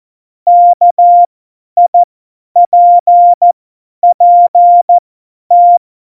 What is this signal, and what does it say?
Morse code, heard as a keyed tone, about 14 wpm, spaced as for 10 wpm, 700 Hz. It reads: KIPPT
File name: Morsekurs / Witze / Witz 10. Oktober 17, Tempo 50